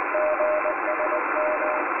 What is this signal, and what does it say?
Morse code, heard as a keyed tone, about 20 wpm, 610 Hz, with radio noise. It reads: GSM